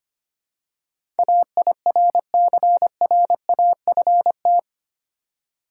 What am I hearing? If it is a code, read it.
Morse code, heard as a keyed tone, about 25 wpm, 700 Hz. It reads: AIRCRAFT
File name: 8 Letter Words - First 3 Letters Given - 25wpm